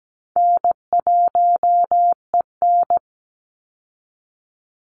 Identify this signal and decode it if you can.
Morse code, heard as a keyed tone, about 17 wpm, 700 Hz. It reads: N1EN
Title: Call Signs Repeated 3x - 17wpm